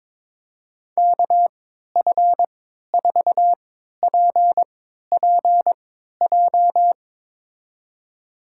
Morse code, heard as a keyed tone, about 22 wpm, spaced as for 12 wpm, 700 Hz. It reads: KF4PPJ
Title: USA Call Signs - 6 Characters - 3x Character Spacing - 22wpm